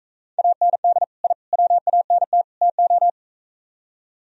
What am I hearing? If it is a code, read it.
Morse code, heard as a keyed tone, about 41 wpm, 700 Hz. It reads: AND I WANT TO